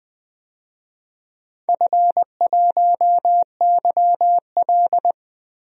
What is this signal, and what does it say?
Morse code, heard as a keyed tone, about 20 wpm, 700 Hz. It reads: F1YL
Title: DXCC-100 Call Signs - 4 Characters - Repeated 3x - 20wpm